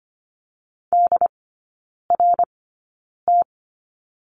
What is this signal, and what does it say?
Morse code, heard as a keyed tone, about 25 wpm, spaced as for 9 wpm, 690 Hz. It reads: DRT
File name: Lesson 14e - Number 2 - Three Characters 25wpm